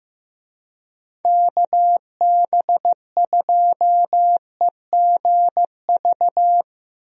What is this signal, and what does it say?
Morse code, heard as a keyed tone, about 15 wpm, 700 Hz. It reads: KB2EGV
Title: USA Call Signs - 6 Characters - Repeated 3x - 15wpm